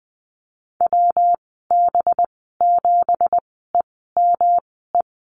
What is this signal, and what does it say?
Morse code, heard as a keyed tone, about 20 wpm, spaced as for 15 wpm, 700 Hz. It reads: WB7EME
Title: 186 Lesson 30g - Letter b - Call Signs 20wpm